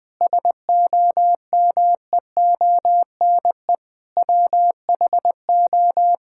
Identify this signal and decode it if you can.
Morse code, heard as a keyed tone, about 20 wpm, 700 Hz. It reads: SOMEONE WHO
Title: Sentences with 6-Word Phrases - 20wpm